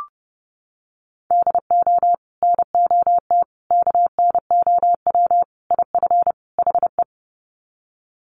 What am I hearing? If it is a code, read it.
Morse code, heard as a keyed tone, about 30 wpm, 700 Hz. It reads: DO NOT KNOW IF HE
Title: Sets of 5 Words Encore2 30wpm